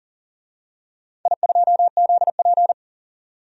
Morse code, heard as a keyed tone, about 40 wpm, 700 Hz. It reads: I2ZP
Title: DXCC-100 Call Signs - 4 Characters - 40wpm